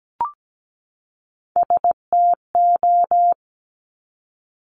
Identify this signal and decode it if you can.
Morse code, heard as a keyed tone, about 17 wpm, 700 Hz. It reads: STO
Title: Top Three Letter Combinations in Words 17wpm